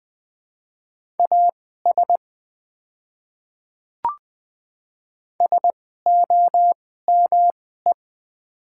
Morse code, heard as a keyed tone, about 20 wpm, spaced as for 14 wpm, 700 Hz. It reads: AS SOME
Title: 2x Character Spacing - Top 200 Words - 20wpm